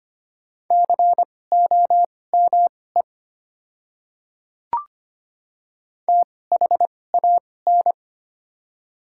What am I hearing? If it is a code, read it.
Morse code, heard as a keyed tone, about 25 wpm, spaced as for 18 wpm, 700 Hz. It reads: COME THAN
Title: 2x Character Spacing - Top 300 Words - 25wpm